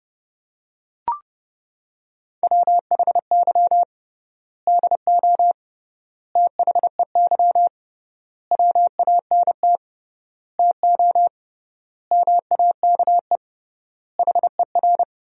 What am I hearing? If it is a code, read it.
Morse code, heard as a keyed tone, about 30 wpm, 700 Hz. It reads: WHY DO THEY WANT TO MAKE HER
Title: 3x Word-Spacing - Sentences from Top 400 Words - 30wpm